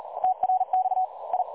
Morse code, heard as a keyed tone, about 38 wpm, 760 Hz. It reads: TRX I
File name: dlazaawansowanych / nagraniazpasma / websdr recording 3565.0kHz